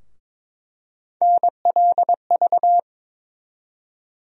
Morse code, heard as a keyed tone, about 22 wpm, 700 Hz. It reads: NLV
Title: USA Call Sign Postfixes - 3 Letters 22wpm